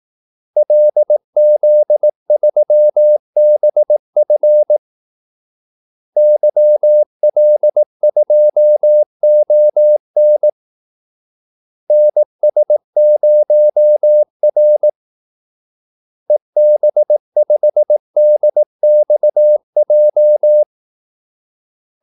Morse code, heard as a keyed tone, about 18 wpm, 600 Hz. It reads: LZ3BF YL2ON NS0R EB5DXJ
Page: Kaldesignaler 18 wpm